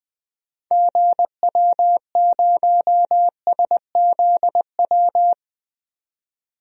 Morse code, heard as a keyed tone, about 20 wpm, 700 Hz. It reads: GW0SZW
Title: DXCC-100 Call Signs - 6 Characters - 20wpm